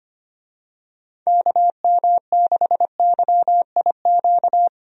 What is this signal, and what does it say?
Morse code, heard as a keyed tone, about 25 wpm, 700 Hz. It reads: KM6YIQ
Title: USA Call Signs - 6 Characters - Repeated 2x - 25wpm